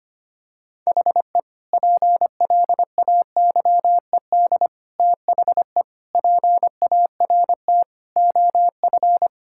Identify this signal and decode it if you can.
Morse code, heard as a keyed tone, about 25 wpm, 700 Hz. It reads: HE PLAYED THE PART OF
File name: practice / 25wpm / Sentences with 4-Word Phrases - 25wpm